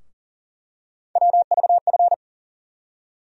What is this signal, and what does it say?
Morse code, heard as a keyed tone, about 40 wpm, 700 Hz. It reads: WVF
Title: USA Call Sign Postfixes - 3 Letters 40wpm